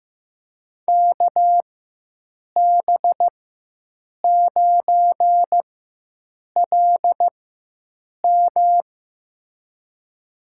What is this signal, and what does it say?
Morse code, heard as a keyed tone, about 15 wpm, spaced as for 7 wpm, 700 Hz. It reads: KB9LM
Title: USA Call Signs - 5 Characters - 4x Character Spacing - 15wpm